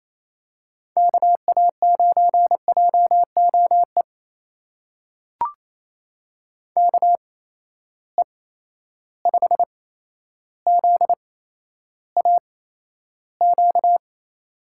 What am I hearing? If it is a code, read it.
Morse code, heard as a keyed tone, about 28 wpm, spaced as for 12 wpm, 700 Hz. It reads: KA9JOE KE5ZAQ
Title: USA Call Signs - 6 Characters - 8x Character Spacing - 28wpm